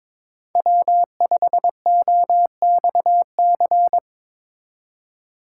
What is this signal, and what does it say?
Morse code, heard as a keyed tone, about 24 wpm, 700 Hz. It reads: W5OXC
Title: Speed Racing - USA 1x3 Call Signs 22wpm